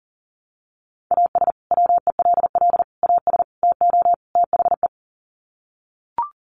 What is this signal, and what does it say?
Morse code, heard as a keyed tone, about 40 wpm, 700 Hz. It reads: AS WELL AS TO THE
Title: Sets of 5 Words Encore2 40wpm